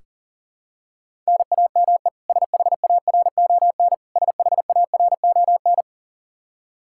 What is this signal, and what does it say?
Morse code, heard as a keyed tone, about 40 wpm, 700 Hz. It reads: NAME SHARON SHARON
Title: Practice QSO - Exchange 1 - Beginning - 40wpm